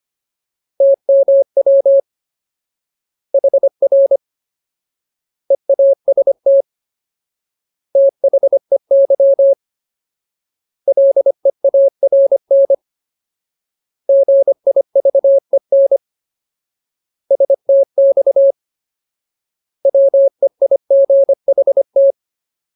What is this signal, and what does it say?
Morse code, heard as a keyed tone, about 25 wpm, 550 Hz. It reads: TMW HR EAST THEY LEARN GIVEN STX WEIGHT
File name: Ultimate Word List / 25WPM Ultimate